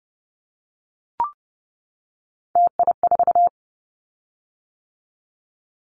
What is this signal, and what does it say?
Morse code, heard as a keyed tone, about 30 wpm, 700 Hz. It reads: TI4
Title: Callsign Prefixes for Top 100 DXCC - 30wpm